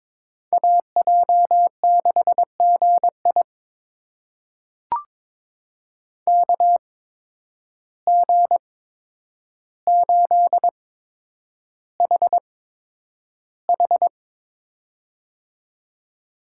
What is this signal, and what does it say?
Morse code, heard as a keyed tone, about 22 wpm, spaced as for 9 wpm, 700 Hz. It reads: AJ6GI KG8HH
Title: USA Call Signs - 5 Characters - 8x Character Spacing - 22wpm